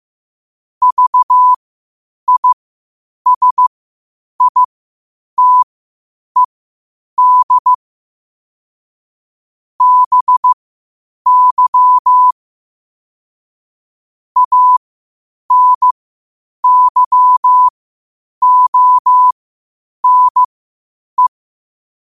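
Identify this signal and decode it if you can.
Morse code, heard as a keyed tone, about 15 wpm, spaced as for 8 wpm, 1000 Hz. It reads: VISITED BY ANYONE
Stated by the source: Morse code at 8 wpm.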